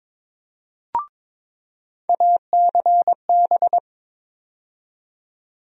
Morse code, heard as a keyed tone, about 22 wpm, 700 Hz. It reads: ACB
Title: USA Call Sign Postfixes - 3 Letters 22wpm